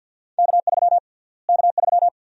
Morse code, heard as a keyed tone, about 50 wpm, 700 Hz. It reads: K3 K3
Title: USA Call Sign Prefixes - Repeated 2x - 50wpm